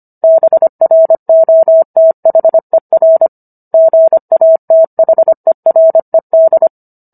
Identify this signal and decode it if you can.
Morse code, heard as a keyed tone, about 25 wpm, 650 Hz. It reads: BROTHER GATHERED